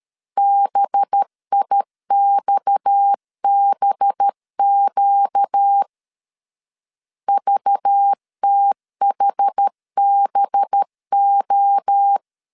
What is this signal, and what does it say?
Morse code, heard as a keyed tone, about 13 wpm, 800 Hz. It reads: BIXBQ VTHBO
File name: LEZIONI / 6 PKX0 / 6 PKX 12wpm 2024